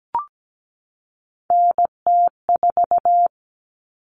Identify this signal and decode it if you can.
Morse code, heard as a keyed tone, about 17 wpm, 700 Hz. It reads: NT4
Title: Common Cut Numbers - 3-Digit 17wpm